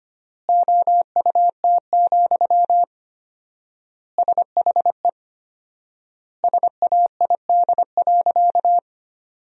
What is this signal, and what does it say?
Morse code, heard as a keyed tone, about 25 wpm, 700 Hz. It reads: OUT, SHE SAID.
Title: 4x Word-Spacing - Sentences from Top 500 Words - 25wpm